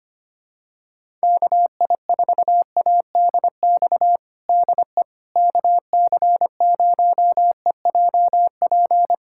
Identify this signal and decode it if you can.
Morse code, heard as a keyed tone, about 25 wpm, 700 Hz. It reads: KI4ADX DE KC0EJP
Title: USA Callsign DE Callsign - 6 Characters - 25wpm